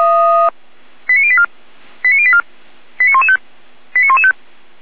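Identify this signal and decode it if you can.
Five-tone selective call (ZVEI), heard as quick runs of five tones, tones of about 70 ms. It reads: 89084 89084 89115 89115
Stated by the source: ZVEI five-tone alert received by SDR.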